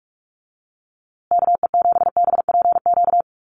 Morse code, heard as a keyed tone, about 44 wpm, 700 Hz. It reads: KE7BPQ
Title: USA Call Signs Repeated 2x 45wpm